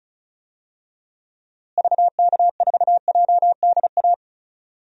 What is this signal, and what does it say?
Morse code, heard as a keyed tone, about 36 wpm, 700 Hz. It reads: VK4JDA
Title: DXCC-100 Call Signs - 6 Characters - Repeated 3x - 35wpm